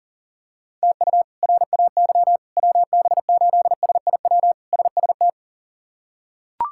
Morse code, heard as a keyed tone, about 40 wpm, 700 Hz. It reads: TU RAY WB8SIW SST
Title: K1USN SST QSO Practice - 40wpm